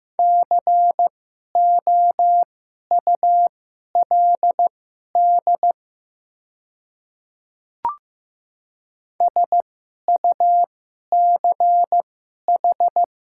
Morse code, heard as a keyed tone, about 15 wpm, spaced as for 11 wpm, 700 Hz. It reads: COULD SUCH
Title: 2x Character Spacing - Top 200 Words - 15wpm